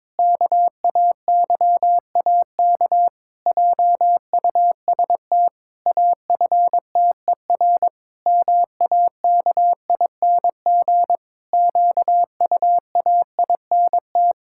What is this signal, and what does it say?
Morse code, heard as a keyed tone, about 22 wpm, 700 Hz. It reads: KAYAK JUST AFTER MAKING QUAINT
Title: Pangram Challenge v2 - 22wpm